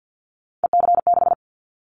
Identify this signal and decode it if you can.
Morse code, heard as a keyed tone, about 50 wpm, 690 Hz. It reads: EC6
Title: Callsign Prefixes for Top 100 DXCC - 50wpm